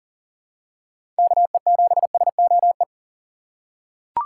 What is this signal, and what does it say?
Morse code, heard as a keyed tone, about 39 wpm, 700 Hz. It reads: KE7SOE
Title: USA Call Signs - 2x Character Spacing - 40wpm